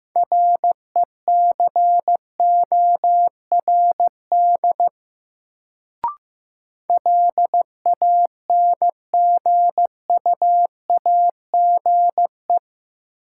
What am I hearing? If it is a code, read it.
Morse code, heard as a keyed tone, about 15 wpm, 700 Hz. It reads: RECORD LANGUAGE
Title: Sentences from Top 500 Words - New Word Drill - 15wpm